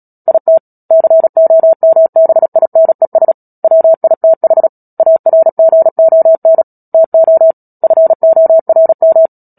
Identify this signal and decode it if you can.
Morse code, heard as a keyed tone, about 37 wpm, 650 Hz. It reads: IT COMBINES WITH ARGON TO FORM
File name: Book Two / 10-Chapter 10 - Epilogue - 36 WPM